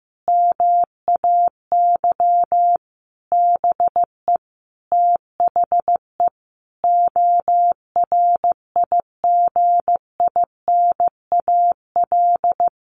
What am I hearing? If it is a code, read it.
Morse code, heard as a keyed tone, about 15 wpm, 700 Hz. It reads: MAY BE THE ORIGINAL
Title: Sentences from Top 900 Words 15wpm